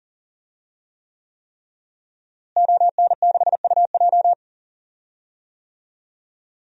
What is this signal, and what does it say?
Morse code, heard as a keyed tone, about 40 wpm, 700 Hz. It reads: ON6UJ
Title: DXCC-100 Call Signs - 5 Characters - Repeated 3x - 40wpm